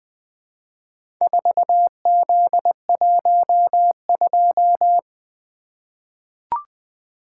Morse code, heard as a keyed tone, about 20 wpm, 700 Hz. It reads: 4Z12
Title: Four Letter-Number Combinations - 20wpm